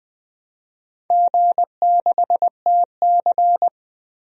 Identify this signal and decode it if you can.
Morse code, heard as a keyed tone, about 20 wpm, 700 Hz. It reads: G6TC
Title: DXCC-100 Call Signs - 4 Characters - 20wpm